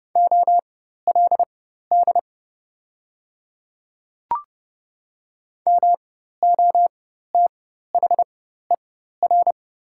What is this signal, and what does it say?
Morse code, heard as a keyed tone, about 30 wpm, spaced as for 14 wpm, 700 Hz. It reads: OLD MOTHER
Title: 4x Character Spacing - Top 200 Words - 30wpm